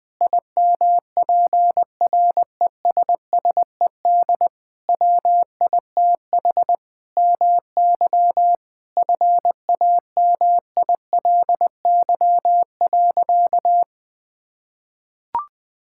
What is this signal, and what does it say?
Morse code, heard as a keyed tone, about 20 wpm, 700 Hz. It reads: IMPRESSED WITH MY FAMILY.
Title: Sentences from Top 1600 Words 20wpm